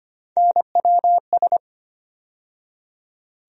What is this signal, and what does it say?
Morse code, heard as a keyed tone, about 25 wpm, 700 Hz. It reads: NWS
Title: USA Call Sign Postfixes - 3 Letters 25wpm